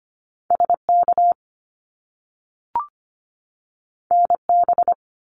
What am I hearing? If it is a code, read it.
Morse code, heard as a keyed tone, about 25 wpm, 700 Hz. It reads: SK NB
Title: Canadian Provinces and Territories 25wpm